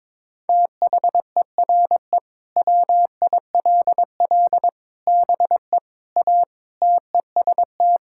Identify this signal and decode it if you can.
Morse code, heard as a keyed tone, about 22 wpm, 700 Hz. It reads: THERE WILL BE A TEST
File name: practice / 22wpm / Sentences with 3-Word Phrases - 22wpm